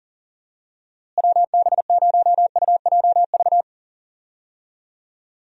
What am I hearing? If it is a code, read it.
Morse code, heard as a keyed tone, about 40 wpm, 700 Hz. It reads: WB0UJV
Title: USA Call Signs - 6 Characters - Repeated 2x - 40wpm